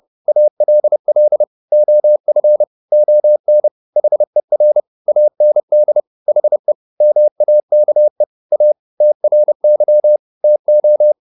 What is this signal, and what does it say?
Morse code, heard as a keyed tone, about 30 wpm, 600 Hz. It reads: ALL OF ON HER AND HE MAKE A TRY TO